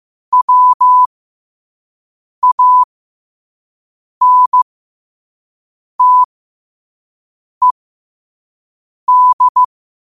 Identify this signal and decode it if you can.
Morse code, heard as a keyed tone, about 15 wpm, spaced as for 5 wpm, 1000 Hz. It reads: WANTED
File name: qotdfiles / 5wpm